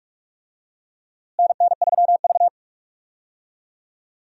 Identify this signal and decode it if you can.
Morse code, heard as a keyed tone, about 46 wpm, 700 Hz. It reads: NN3V
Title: USA Call Signs - 4 Characters - Repeated 2x - 45wpm